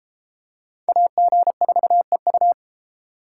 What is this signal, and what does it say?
Morse code, heard as a keyed tone, about 31 wpm, 700 Hz. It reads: AG4EU